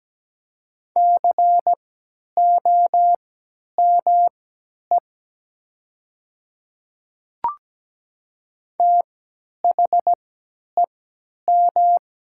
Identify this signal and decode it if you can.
Morse code, heard as a keyed tone, about 17 wpm, spaced as for 10 wpm, 700 Hz. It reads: COME THEM
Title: 3x Character Spacing - Top 100 Words - 17wpm